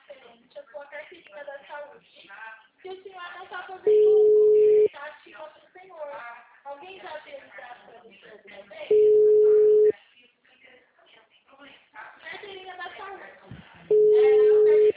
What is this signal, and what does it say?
Ringback tone (Europe, 425 Hz), with speech.
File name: out- -2009- - - .79290